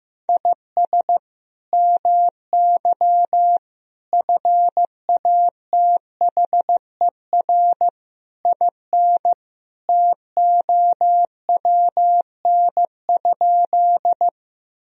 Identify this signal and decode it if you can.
Morse code, heard as a keyed tone, about 15 wpm, 700 Hz. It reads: IS MY FATHER IN TOWN?
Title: Sentences from Top 1000 Words - Review Part 2 - 15wpm